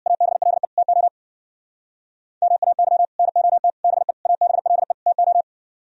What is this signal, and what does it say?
Morse code, heard as a keyed tone, about 63 wpm, 700 Hz. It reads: ABLE TO MAY NOT BE ABLE TO